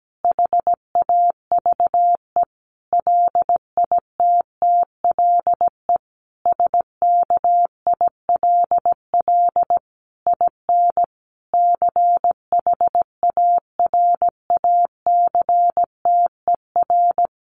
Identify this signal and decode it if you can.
Morse code, heard as a keyed tone, about 17 wpm, 700 Hz. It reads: HAVE LITTLE SKILL IN CHARACTER
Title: Sentences from Top 1200 Words 17wpm